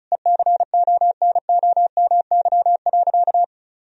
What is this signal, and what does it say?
Morse code, heard as a keyed tone, about 36 wpm, 700 Hz. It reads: ECONOMY.
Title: Sentences from Top 1000 Words - Review Part 4 - 35wpm